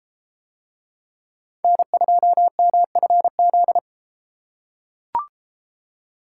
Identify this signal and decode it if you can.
Morse code, heard as a keyed tone, about 31 wpm, 700 Hz. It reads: N2MFZ